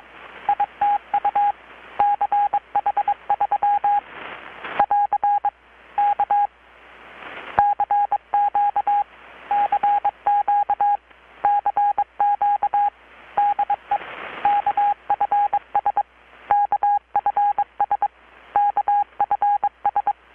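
Morse code, heard as a keyed tone, about 22 wpm, 820 Hz. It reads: ITU CH3 <AR> K CQ CQ CQ DE KFS KFS KFS
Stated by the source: Morse from KPH's Night of Nights XX maritime radiotelegraphy event, received in Chiapas, Mexico about 3,800 km away on a Kenwood R-600 with a grounded 10 m long-wire antenna.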